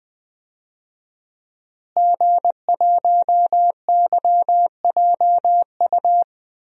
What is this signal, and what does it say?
Morse code, heard as a keyed tone, about 20 wpm, 700 Hz. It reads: G1YJU
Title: DXCC-100 Call Signs - 5 Characters - Repeated 3x - 20wpm